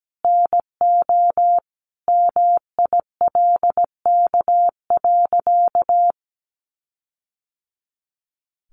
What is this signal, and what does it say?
Morse code, heard as a keyed tone, about 17 wpm, 700 Hz. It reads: NO MILK.
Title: Sentences from Top 1500 Words 17wpm